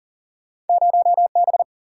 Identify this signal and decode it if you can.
Morse code, heard as a keyed tone, about 42 wpm, 700 Hz. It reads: 0B
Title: Single-Letter Single-Number 40wpm